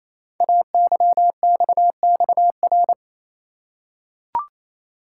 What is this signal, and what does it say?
Morse code, heard as a keyed tone, about 27 wpm, 700 Hz. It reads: AYXXR
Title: Code Groups - 5 Letters - 28wpm